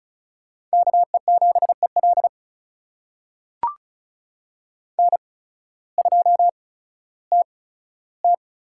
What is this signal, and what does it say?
Morse code, heard as a keyed tone, about 36 wpm, spaced as for 16 wpm, 700 Hz. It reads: KE7EL N2TT
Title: USA Call Signs - 5 Characters - 8x Character Spacing - 35wpm